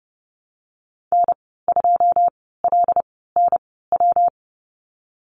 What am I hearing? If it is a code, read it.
Morse code, heard as a keyed tone, about 30 wpm, spaced as for 17 wpm, 700 Hz. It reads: N2LNW
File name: Lesson 19g - Letter w - Call Signs 30wpm